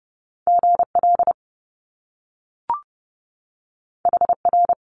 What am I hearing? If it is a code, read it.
Morse code, heard as a keyed tone, about 30 wpm, 700 Hz. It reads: GL HR
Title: Top Two Letter Combinations in Words 30wpm